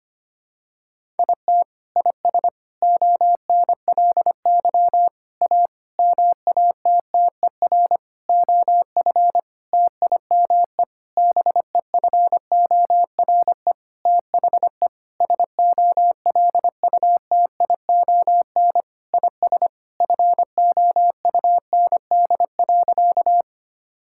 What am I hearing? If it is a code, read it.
Morse code, heard as a keyed tone, about 25 wpm, 700 Hz. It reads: IT IS ONLY A MATTER OF TIME BEFORE THE SOLUTION IS FOUND.
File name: practice / 25wpm / Sentences with 9-Word Phrases - 25wpm